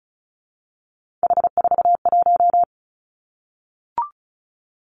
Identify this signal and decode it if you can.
Morse code, heard as a keyed tone, about 36 wpm, 700 Hz. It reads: H41
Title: Three Challenging Letter-Number Combinations 35wpm